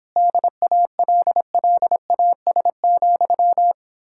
Morse code, heard as a keyed tone, about 31 wpm, 700 Hz. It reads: DALLAS,